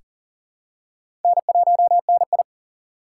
Morse code, heard as a keyed tone, about 40 wpm, 700 Hz. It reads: N1NI
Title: USA Call Signs - 4 Characters - 2x Character Spacing - 40wpm